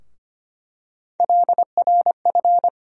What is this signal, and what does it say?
Morse code, heard as a keyed tone, about 25 wpm, 700 Hz. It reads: LRF
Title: USA Call Sign Postfixes - 3 Letters 25wpm